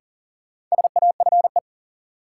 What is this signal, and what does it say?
Morse code, heard as a keyed tone, about 40 wpm, 700 Hz. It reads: SAFE
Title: Sentences from Top 900 Words - New Word Drill - 40wpm